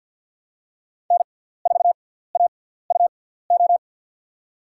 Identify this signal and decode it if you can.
Morse code, heard as a keyed tone, about 52 wpm, spaced as for 17 wpm, 700 Hz. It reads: N4AUO
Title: USA Call Signs - 5 Characters - 6x Character Spacing - 50wpm